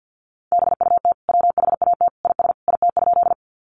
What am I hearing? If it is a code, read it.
Morse code, heard as a keyed tone, about 50 wpm, 700 Hz. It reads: BUT WHAT IS IT?
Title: Sentences from Top 100 Words 50wpm